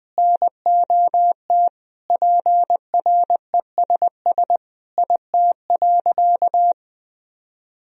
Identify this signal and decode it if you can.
Morse code, heard as a keyed tone, about 21 wpm, 700 Hz. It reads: NOT PRESS IT.
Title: Sentences from Top 1000 Words - Review Part 1 - 20wpm